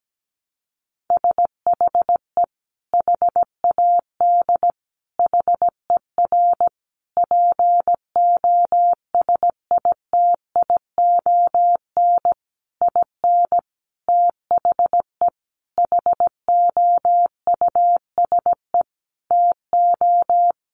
Morse code, heard as a keyed tone, about 17 wpm, 700 Hz. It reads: SHE HAD HER POSITION IN THE HOUSE TO
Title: Sentences from Top 900 Words 17wpm